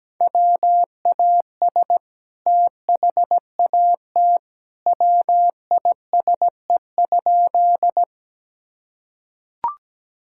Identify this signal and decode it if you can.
Morse code, heard as a keyed tone, about 17 wpm, 700 Hz. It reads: WAS THAT WISE?
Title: Sentences from Top 1600 Words 17wpm